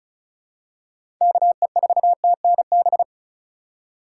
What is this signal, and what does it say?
Morse code, heard as a keyed tone, about 36 wpm, 700 Hz. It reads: KE4TNB
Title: USA Call Signs - 3x Character Spacing - 35wpm